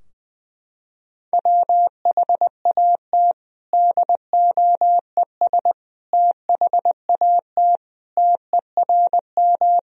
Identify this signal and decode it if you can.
Morse code, heard as a keyed tone, about 20 wpm, 700 Hz. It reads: WHAT DOES THAT TERM
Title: Sentences from Top 1000 Words - Review Part 4 - 20wpm